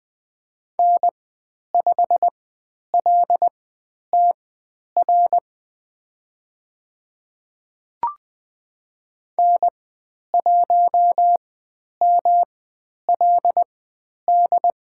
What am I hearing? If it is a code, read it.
Morse code, heard as a keyed tone, about 20 wpm, spaced as for 10 wpm, 700 Hz. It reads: N5LTR N1MLD
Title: 109 Lesson 18g - Letter m - Call Signs 20wpm